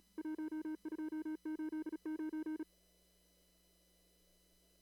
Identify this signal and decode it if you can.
Morse code, heard as a keyed tone, about 34 wpm, 310 Hz. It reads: 1289
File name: CW / kt5j wpx cw-op-k5pi